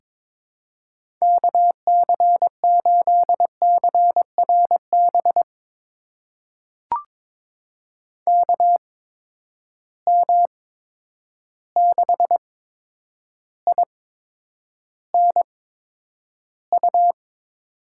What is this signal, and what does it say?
Morse code, heard as a keyed tone, about 21 wpm, spaced as for 9 wpm, 700 Hz. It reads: KC8CRB KM6INU
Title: USA Call Signs - 8x Character Spacing - 22wpm